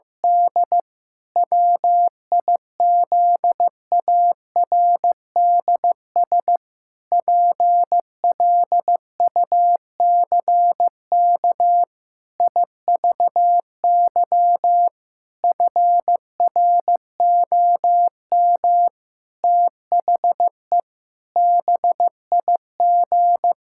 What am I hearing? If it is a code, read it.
Morse code, heard as a keyed tone, about 15 wpm, 700 Hz. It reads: D WIZARDS PLUCK IVY FROM THE BIG
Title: Pangram Challenge v2 - 15wpm